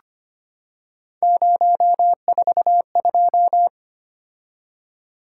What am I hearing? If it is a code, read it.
Morse code, heard as a keyed tone, about 25 wpm, 700 Hz. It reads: 042
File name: Three Letter-Number Combinations - 25wpm